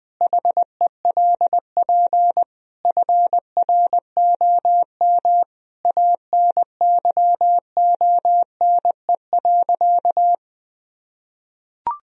Morse code, heard as a keyed tone, about 21 wpm, 700 Hz. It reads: HELP FROM ANYONE.
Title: Sentences from Top 1000 Words - Review Part 3 - 20wpm